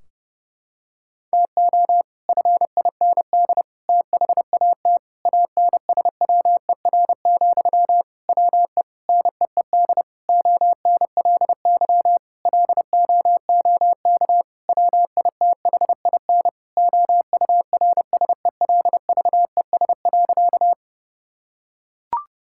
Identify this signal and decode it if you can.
Morse code, heard as a keyed tone, about 29 wpm, 700 Hz. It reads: TO FIND THAT ANSWER, WE NEED ONLY LOOK WITHIN OURSELVES.
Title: Sentences from Top 1600 Words 30wpm